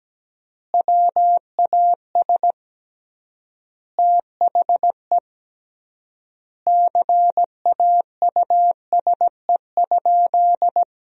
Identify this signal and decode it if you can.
Morse code, heard as a keyed tone, about 17 wpm, 700 Hz. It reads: WAS THE CAUSE?
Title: 3x Word-Spacing - Sentences from Top 200 Words - 17wpm